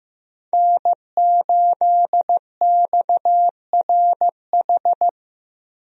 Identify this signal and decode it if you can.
Morse code, heard as a keyed tone, about 15 wpm, 700 Hz. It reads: N8XRH
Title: USA Call Signs - 5 Characters - Repeated 3x - 15wpm